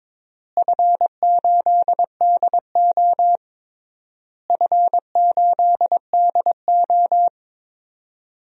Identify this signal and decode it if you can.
Morse code, heard as a keyed tone, about 22 wpm, 700 Hz. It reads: F8DO F8DO
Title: DXCC-100 Call Signs - 4 Characters - Repeated 3x - 22wpm